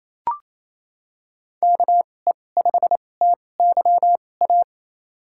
Morse code, heard as a keyed tone, about 28 wpm, spaced as for 20 wpm, 700 Hz. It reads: KE5TYA
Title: USA Call Signs - 6 Characters - 2x Character Spacing - 28wpm